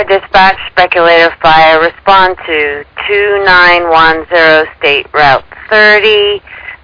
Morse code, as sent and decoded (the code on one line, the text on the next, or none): none